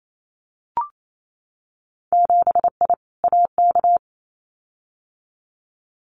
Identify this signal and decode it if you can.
Morse code, heard as a keyed tone, about 28 wpm, 700 Hz. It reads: 7I AK
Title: Winter Field Day Exchanges Encore 28wpm